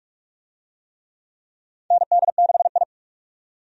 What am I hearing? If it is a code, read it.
Morse code, heard as a keyed tone, about 46 wpm, 700 Hz. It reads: ND6I
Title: USA Call Signs - 4 Characters - 45wpm